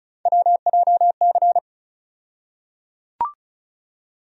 Morse code, heard as a keyed tone, about 34 wpm, 700 Hz. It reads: WJC
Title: USA Call Sign Postfixes - 3 Letters 35wpm